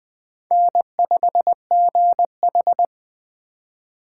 Morse code, heard as a keyed tone, about 20 wpm, 700 Hz. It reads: N5GH
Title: USA Call Signs - 4 Characters - 4x Character Spacing - 20wpm